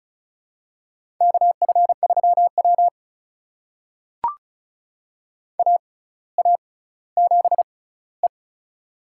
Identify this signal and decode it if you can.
Morse code, heard as a keyed tone, about 34 wpm, spaced as for 18 wpm, 700 Hz. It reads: KF3W AA7E
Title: USA Call Signs - 4 Characters - 6x Character Spacing - 35wpm